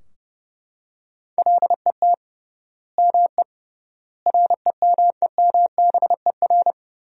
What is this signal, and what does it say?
Morse code, heard as a keyed tone, about 31 wpm, 700 Hz. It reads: LET ME REMEMBER
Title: 3x Word-Spacing - Sentences from Top 400 Words - 30wpm